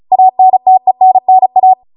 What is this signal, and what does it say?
Morse code, heard as a keyed tone, about 36 wpm, 750 Hz. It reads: ANTENNA